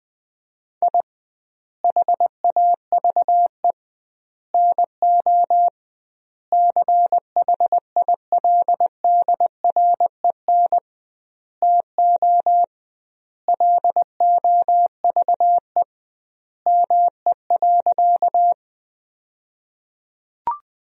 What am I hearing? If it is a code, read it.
Morse code, heard as a keyed tone, about 20 wpm, 700 Hz. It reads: I HAVE NO CHILDREN TO LOVE ME.
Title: 2x Word-Spacing - Sentences from Top 400 Words - 20wpm